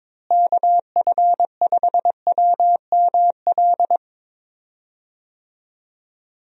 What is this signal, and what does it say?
Morse code, heard as a keyed tone, about 22 wpm, 700 Hz. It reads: KF5WML
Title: USA Call Signs - 6 Characters - Repeated 3x - 22wpm